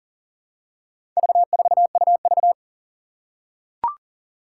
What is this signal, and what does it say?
Morse code, heard as a keyed tone, about 40 wpm, 700 Hz. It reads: V4UV
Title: Four Challenging Letter-Number Combinations 40wpm